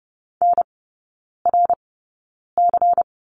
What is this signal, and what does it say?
Morse code, heard as a keyed tone, about 30 wpm, spaced as for 9 wpm, 700 Hz. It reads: NRC
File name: Lesson 16e - Letter c - Three Characters 30wpm